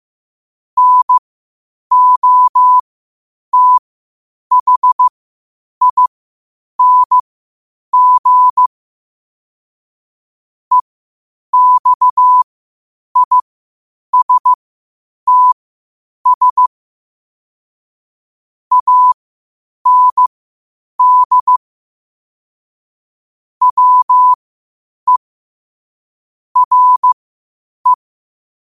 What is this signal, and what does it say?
Morse code, heard as a keyed tone, about 15 wpm, spaced as for 8 wpm, 1000 Hz. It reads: NOTHING EXISTS AND WERE
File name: qotdfiles / 8wpm